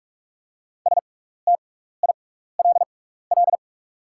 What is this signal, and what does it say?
Morse code, heard as a keyed tone, about 44 wpm, spaced as for 15 wpm, 700 Hz. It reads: STILL
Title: 6x Character Spacing - Top 300 Words - 45wpm